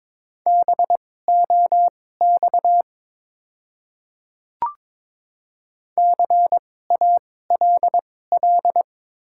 Morse code, heard as a keyed tone, about 22 wpm, spaced as for 16 wpm, 700 Hz. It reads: BOX CALL
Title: 2x Character Spacing - Top 500 Words - 22wpm